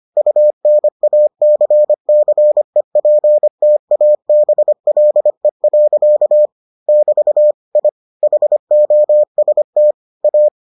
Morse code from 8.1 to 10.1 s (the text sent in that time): HOST